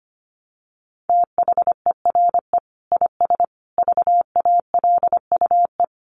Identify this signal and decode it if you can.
Morse code, heard as a keyed tone, about 25 wpm, 700 Hz. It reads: THERE IS VALUE
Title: Sentences from Top 1100 Words 25wpm